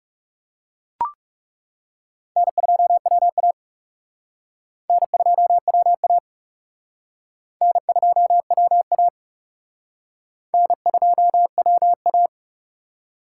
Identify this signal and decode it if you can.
Morse code, heard as a keyed tone, about 38 wpm, 700 Hz. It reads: N2WA N2WA N2WA N2WA